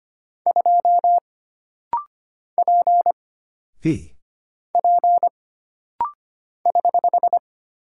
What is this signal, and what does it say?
Morse code, heard as a keyed tone, about 25 wpm, 700 Hz. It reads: 2 P P <HH>
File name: Single Character (Letter:Number:Punctuation:Prosign) - Rapid-Fire - 25wpm